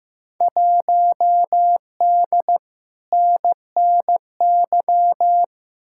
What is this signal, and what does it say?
Morse code, heard as a keyed tone, about 15 wpm, 700 Hz. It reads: 1D NNY
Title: Field Day Exchanges - 15wpm